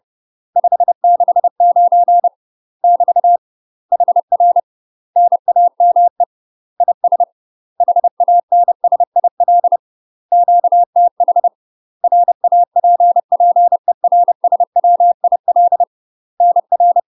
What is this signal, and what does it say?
Morse code, heard as a keyed tone, about 30 wpm, 700 Hz. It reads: 569 <BT> HR NAME IS HANSIL QTH RAPPERSWIL NR
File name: UK MASTER QSOs TRAINING / 30WPM / UK MASTER QSOs 0038